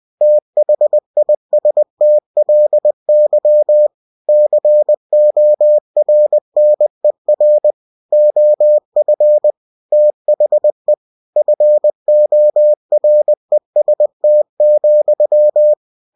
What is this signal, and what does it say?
Morse code, heard as a keyed tone, about 20 wpm, 600 Hz. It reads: THISTLY CORNER OF THE FOREST,
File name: cw-books / winnie-the-pooh / winnie-20wpm-page-0026